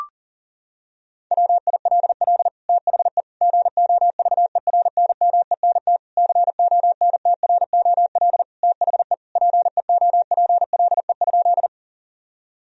Morse code, heard as a keyed tone, about 40 wpm, 700 Hz. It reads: WILL THE GOVERNMENT CONTROL THE PEOPLE?
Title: Sentences from Top 1000 Words - Review Part 3 - 40wpm